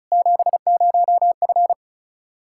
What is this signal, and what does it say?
Morse code, heard as a keyed tone, about 36 wpm, 700 Hz. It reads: 70F
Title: Practice QSO - Exchange 2 - Info - 35wpm